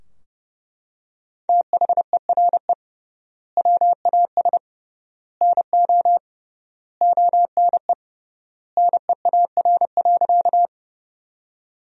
Morse code, heard as a keyed tone, about 30 wpm, 700 Hz. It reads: THERE WAS NO ONE NEAR.
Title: 3x Word-Spacing - Sentences from Top 200 Words - 30wpm